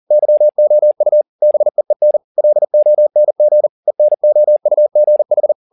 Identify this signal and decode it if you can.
Morse code, heard as a keyed tone, about 40 wpm, 600 Hz. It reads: YOU BEEN LONG ENOUGH